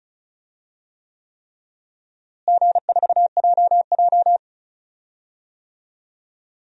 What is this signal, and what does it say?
Morse code, heard as a keyed tone, about 36 wpm, 700 Hz. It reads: G4JJ
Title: DXCC-100 Call Signs - 4 Characters - Repeated 3x - 35wpm